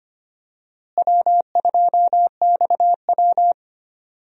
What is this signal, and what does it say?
Morse code, heard as a keyed tone, about 25 wpm, 700 Hz. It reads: W2XW
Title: USA Call Signs - 4 Characters - Repeated 3x - 25wpm